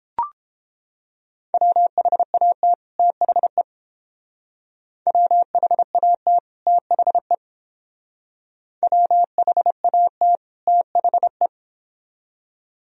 Morse code, heard as a keyed tone, about 30 wpm, 700 Hz. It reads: WHAT THE WHAT THE WHAT THE